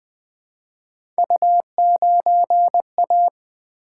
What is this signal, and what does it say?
Morse code, heard as a keyed tone, about 20 wpm, 700 Hz. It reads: U9A
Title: Three Letter-Number Combinations - 20wpm